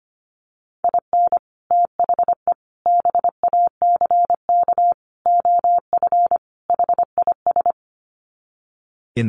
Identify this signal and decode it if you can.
Morse code, heard as a keyed tone, about 25 wpm, 700 Hz. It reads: IN THE BACK OF HIS
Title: Sets of 5 Words Encore2 25wpm